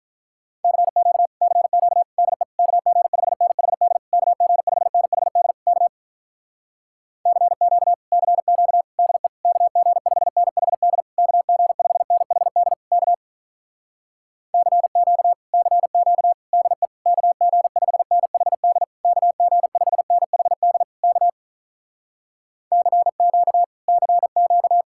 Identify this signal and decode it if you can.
Morse code, heard as a keyed tone, about 46 wpm, 700 Hz. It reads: CQ CQ DE KG5NHD KG5NHD K CQ CQ DE KG5NHD KG5NHD K CQ CQ DE KG5NHD KG5NHD K CQ CQ